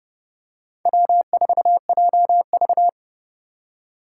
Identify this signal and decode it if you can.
Morse code, heard as a keyed tone, about 29 wpm, 700 Hz. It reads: W4JV
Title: USA Call Signs - 4 Characters - 3x Character Spacing - 30wpm